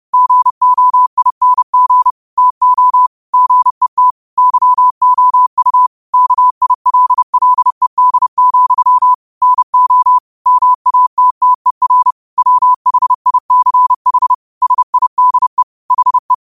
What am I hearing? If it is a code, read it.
Morse code, heard as a keyed tone, about 30 wpm, 1000 Hz. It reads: GOING TO GET YOU KILLED, NO MATTER WHICH SIDE HE
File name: qotdfiles / 30wpm